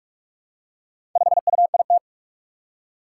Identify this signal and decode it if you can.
Morse code, heard as a keyed tone, about 44 wpm, 700 Hz. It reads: 5UIT
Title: Four Challenging Letter-Number Combinations 45wpm